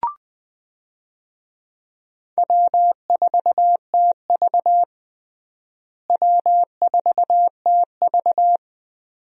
Morse code, heard as a keyed tone, about 21 wpm, 700 Hz. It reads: W4TV W4TV
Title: USA Call Signs - 4 Characters - Repeated 2x - 20wpm